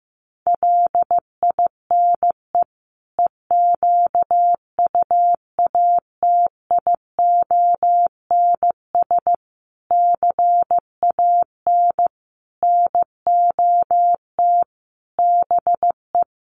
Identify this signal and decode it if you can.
Morse code, heard as a keyed tone, about 15 wpm, 700 Hz. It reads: LINE EQUATIONS CAN NOT BE
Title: Sets of 5 Words 15wpm